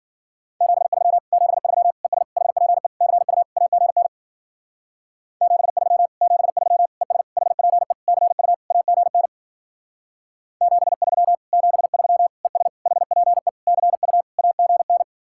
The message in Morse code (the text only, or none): 73 73 ES HPE CU AGN 73 73 ES HPE CU AGN 73 73 ES HPE CU AGN